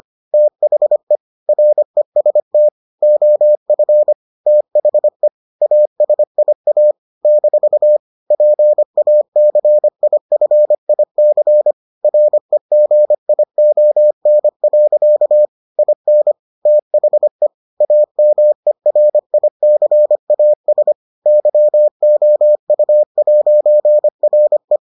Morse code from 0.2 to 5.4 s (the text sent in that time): THE REST OF THE